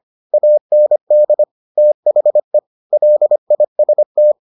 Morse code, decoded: AND THE LIST

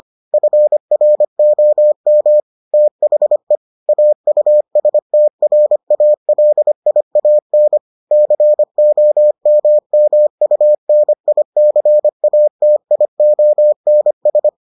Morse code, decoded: FROM THE AUSTRALIAN COMMUNICATIONS

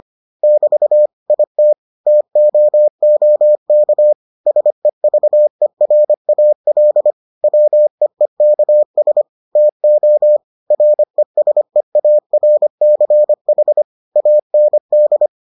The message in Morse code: <BT> IT TOOK SEVERAL WEEKS TO RESEARCH AND

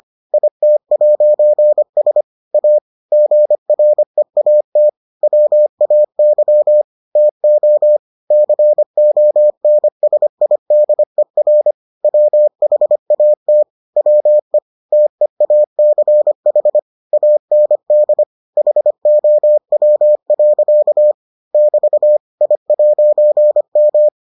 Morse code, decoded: IT'S A GREAT WAY TO CONSIDER WHAT WE TEACH AND HOW. <BT> I'M

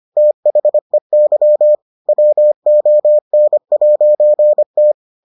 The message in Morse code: THEY WON'T

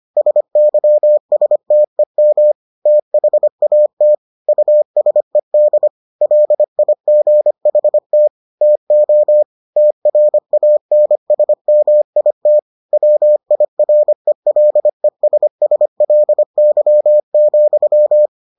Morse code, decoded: SYSTEM THAT USED LIGHT TO TRANSMIT WIRELESSLY,

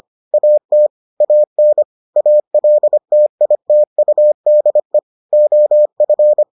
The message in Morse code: AT AN ALTITUDE OF

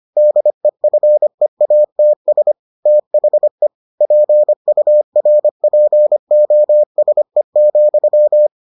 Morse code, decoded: DEFEATS THE PURPOSE,

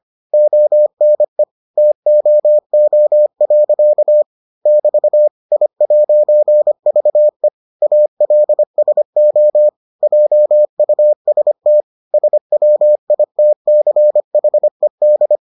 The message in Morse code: ONE TOO. <BT> I'VE ALSO JUST SWITCHED

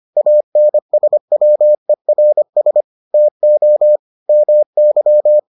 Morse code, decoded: ANSWERS TO MY

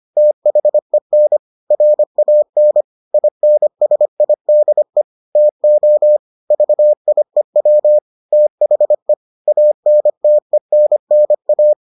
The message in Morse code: THEN RAN INSIDE TO VIEW THE ANTENNA

